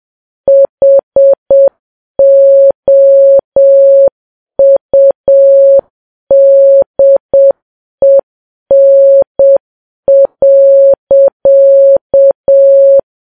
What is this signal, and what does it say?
Morse code, heard as a keyed tone, about 7 wpm, 550 Hz. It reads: HOUDEN.
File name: Files Web tls / Morse on9fv / morse7kps